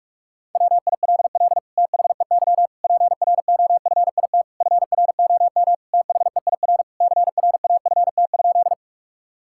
Morse code, heard as a keyed tone, about 41 wpm, 700 Hz. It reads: WILL THEY PROFIT FROM THEIR CRAFT?
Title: Sentences from Top 1000 Words - Review Part 6 - 45wpm